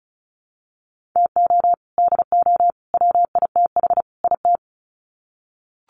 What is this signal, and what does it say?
Morse code, heard as a keyed tone, about 34 wpm, 700 Hz. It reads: TO DO WITH IT
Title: Sets of 4 Words 35wpm